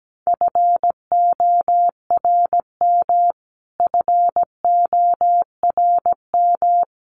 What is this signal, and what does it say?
Morse code, heard as a keyed tone, about 17 wpm, 700 Hz. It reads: FORM FORM
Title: Top 200 Words Repeated 17wpm